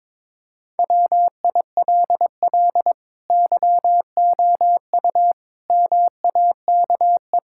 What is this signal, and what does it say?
Morse code, heard as a keyed tone, about 22 wpm, 700 Hz. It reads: WILL YOU MAKE
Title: Sentences from Top 1000 Words - Review Part 6 - 22wpm